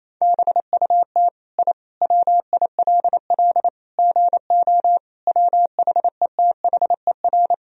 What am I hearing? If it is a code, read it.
Morse code, heard as a keyed tone, about 28 wpm, 690 Hz. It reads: BUT I WILL GO WHETHER
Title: Sentences from Top 1000 Words - Review Part 3 - 28wpm